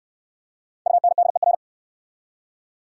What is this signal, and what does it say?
Morse code, heard as a keyed tone, about 70 wpm, 700 Hz. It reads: UNDER